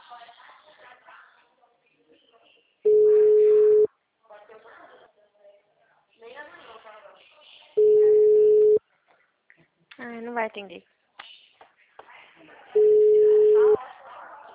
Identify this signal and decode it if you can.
Ringback tone (Europe, 425 Hz), with speech.
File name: out- -2014- - - .47637